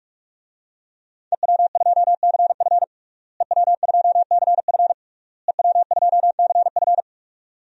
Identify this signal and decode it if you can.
Morse code, heard as a keyed tone, about 41 wpm, 700 Hz. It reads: EW2CF EW2CF EW2CF
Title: DXCC-100 Call Signs - 5 Characters - Repeated 3x - 45wpm